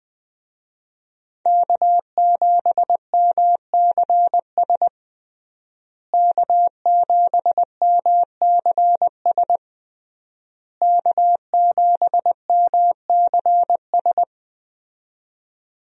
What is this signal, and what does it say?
Morse code, heard as a keyed tone, about 20 wpm, 700 Hz. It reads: K7MCS K7MCS K7MCS
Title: USA Call Signs - 5 Characters - Repeated 3x - 20wpm